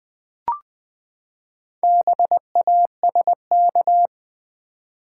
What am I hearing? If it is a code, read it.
Morse code, heard as a keyed tone, about 19 wpm, 700 Hz. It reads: BASK
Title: Build A Word - 4 Letters At A Time - 10-Letter Words - 20wpm